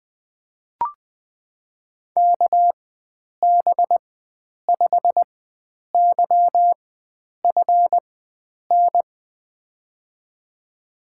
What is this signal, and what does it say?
Morse code, heard as a keyed tone, about 20 wpm, spaced as for 9 wpm, 700 Hz. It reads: KB5YFN
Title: USA Call Signs - 4x Character Spacing - 20wpm